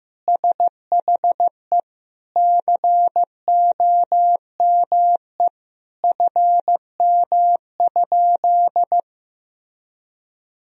Morse code, heard as a keyed tone, about 15 wpm, 700 Hz. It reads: SHE COME FM?
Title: Sentences from Top 200 Words and Abbreviations 15wpm